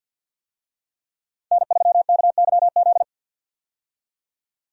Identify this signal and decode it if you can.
Morse code, heard as a keyed tone, about 51 wpm, 700 Hz. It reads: N3KYZ
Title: USA Call Signs - 5 Characters - 50wpm